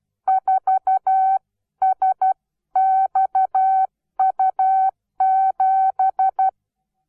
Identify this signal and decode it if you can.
Morse code, heard as a keyed tone, about 12 wpm, 760 Hz. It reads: 4SXU7
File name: Onlinelehrgang / morsen / DJ4UF Morsen17